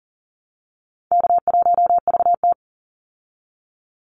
Morse code, heard as a keyed tone, about 40 wpm, 700 Hz. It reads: K1VT
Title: Easy Call Signs Repeated 2x - 40wpm